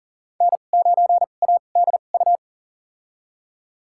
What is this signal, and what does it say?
Morse code, heard as a keyed tone, about 40 wpm, spaced as for 29 wpm, 700 Hz. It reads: N9ADU
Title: USA Call Signs - 5 Characters - 2x Character Spacing - 40wpm